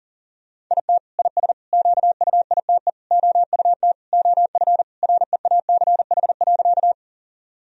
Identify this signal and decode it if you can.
Morse code, heard as a keyed tone, about 40 wpm, 700 Hz. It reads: IT IS QUITE OUT OF REACH.
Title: Sentences from Top 1000 Words - Review Part 2 - 40wpm